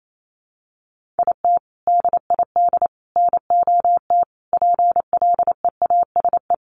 Morse code, heard as a keyed tone, about 28 wpm, 700 Hz. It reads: IT DID NOT PLEASE
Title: Sentences from Top 1500 Words 28wpm